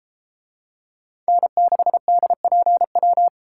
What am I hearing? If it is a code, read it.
Morse code, heard as a keyed tone, about 32 wpm, 700 Hz. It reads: N6DPW